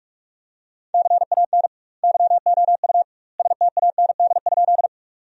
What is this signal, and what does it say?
Morse code, heard as a keyed tone, about 46 wpm, 700 Hz. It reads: CAN YOU STAND?
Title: 2x Word-Spacing - Sentences from Top 300 Words - 45wpm